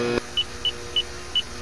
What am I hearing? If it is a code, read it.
There is no code.